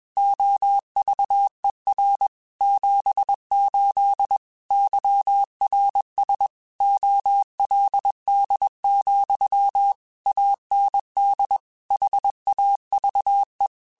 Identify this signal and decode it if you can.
Morse code, heard as a keyed tone, about 21 wpm, 800 Hz. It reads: OVER 78 YRS OLD, AND HAVE
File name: QSOs / QSOs-mp3s / kc7haa@20